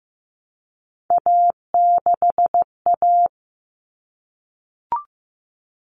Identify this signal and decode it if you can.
Morse code, heard as a keyed tone, about 15 wpm, 700 Hz. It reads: A6A
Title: Common Cut Numbers - 3-Digit 15wpm